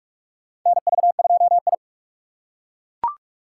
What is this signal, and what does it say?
Morse code, heard as a keyed tone, about 41 wpm, 700 Hz. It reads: NV2I
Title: USA Call Signs - 4 Characters - 4x Character Spacing - 45wpm